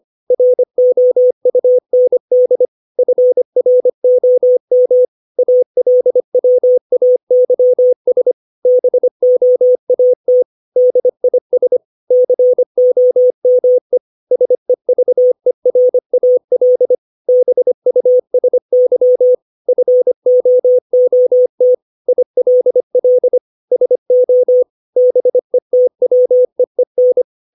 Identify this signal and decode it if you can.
Morse code, heard as a keyed tone, about 25 wpm, 500 Hz. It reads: ROUND FROM ALWAYS BOAT DIS COME SEVERAL BUSY FOOT ILL SO BETWEEN
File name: 632 words / 632 Words 25WPM